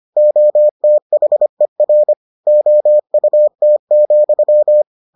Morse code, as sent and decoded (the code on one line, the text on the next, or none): --- - .... .-. / --- ..- - --..--
OTHER OUT,